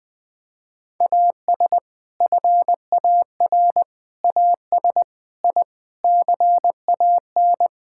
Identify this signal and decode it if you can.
Morse code, heard as a keyed tone, about 20 wpm, 700 Hz. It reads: AS FAR AS I CAN
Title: Sentences with 7-Word Phrases - 20wpm